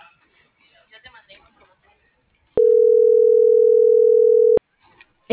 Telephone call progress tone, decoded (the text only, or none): ringback tone